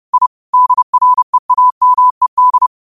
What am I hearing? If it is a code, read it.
Morse code, heard as a keyed tone, about 30 wpm, 1000 Hz. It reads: I DREAMED